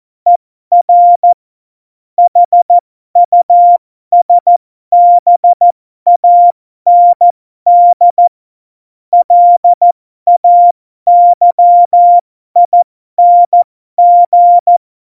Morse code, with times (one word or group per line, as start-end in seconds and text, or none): 0.3-1.3 ER
2.2-8.3 HUSBAND
9.1-14.8 LAYING